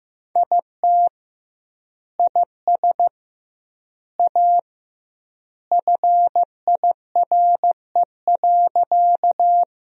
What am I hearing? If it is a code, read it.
Morse code, heard as a keyed tone, about 15 wpm, 700 Hz. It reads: IT IS A FIRE.
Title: 2x Word-Spacing - Sentences from Top 400 Words - 15wpm